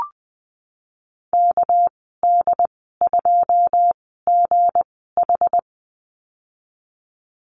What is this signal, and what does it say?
Morse code, heard as a keyed tone, about 20 wpm, spaced as for 15 wpm, 700 Hz. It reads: KD2GH
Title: 200 Lesson 32g - Letter k - Call Signs 20wpm